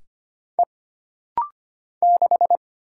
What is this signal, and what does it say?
Morse code, heard as a keyed tone, about 25 wpm, 700 Hz. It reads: E 6
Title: Single Letter-Number Alt - Rapid-Fire 25wpm